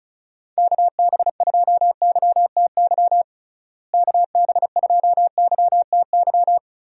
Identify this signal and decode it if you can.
Morse code, heard as a keyed tone, about 34 wpm, 700 Hz. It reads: KB2YTY KB2YTY
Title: USA Call Signs - 6 Characters - Repeated 3x - 35wpm